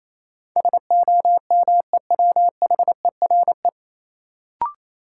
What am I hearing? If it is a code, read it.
Morse code, heard as a keyed tone, about 28 wpm, 700 Hz. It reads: SOMEWHERE
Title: Sentences from Top 1400 Words - New Word Drill Alt - 28wpm